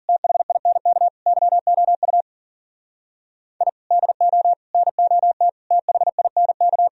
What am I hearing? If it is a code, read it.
Morse code, heard as a keyed tone, about 52 wpm, 700 Hz. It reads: THINK YOU I DO NOT THINK